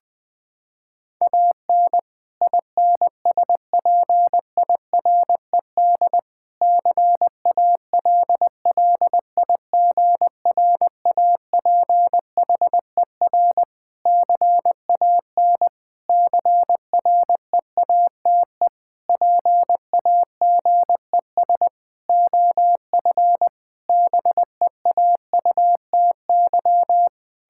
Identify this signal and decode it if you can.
Morse code, heard as a keyed tone, about 20 wpm, 700 Hz. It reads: AN INSPIRED CALLIGRAPHER CAN CREATE PAGES OF BEAUTY
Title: Pangram Challenge v2 - 20wpm